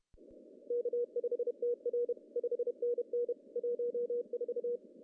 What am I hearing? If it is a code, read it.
Morse code, heard as a keyed tone, about 28 wpm, 480 Hz. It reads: K5TR 5NN 14
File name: CW / cqww-cw-k5tr